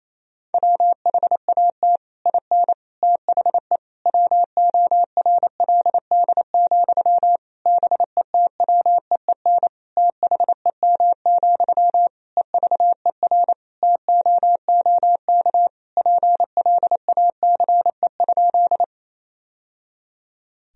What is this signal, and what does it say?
Morse code, heard as a keyed tone, about 28 wpm, 700 Hz. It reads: WHAT IN THE WORLD, BETWEEN THEM, EVER TOOK PLACE?
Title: Sentences from Top 1000 Words - Review Part 1 - 28wpm